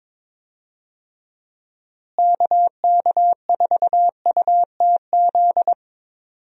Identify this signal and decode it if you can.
Morse code, heard as a keyed tone, about 22 wpm, 700 Hz. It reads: KK4UTZ
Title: USA Call Signs - 6 Characters - Repeated 2x - 22wpm